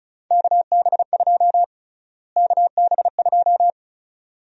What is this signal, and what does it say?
Morse code, heard as a keyed tone, about 34 wpm, 700 Hz. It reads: KB2 KB2
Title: USA Call Sign Prefixes - Repeated 2x - 35wpm